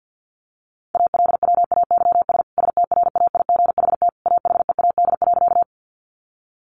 Morse code, heard as a keyed tone, about 50 wpm, 700 Hz. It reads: ALWAYS STRAIGHT AHEAD.
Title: Sentences from Top 1100 Words 50wpm